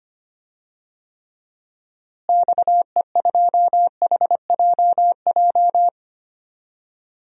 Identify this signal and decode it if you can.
Morse code, heard as a keyed tone, about 25 wpm, 700 Hz. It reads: XE2HJJ
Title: DXCC-100 Call Signs - 6 Characters - Repeated 3x - 25wpm